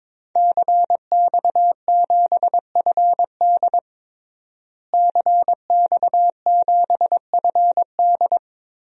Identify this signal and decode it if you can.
Morse code, heard as a keyed tone, about 23 wpm, 700 Hz. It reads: CX7FD CX7FD
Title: DXCC-100 Call Signs - 5 Characters - Repeated 2x - 22wpm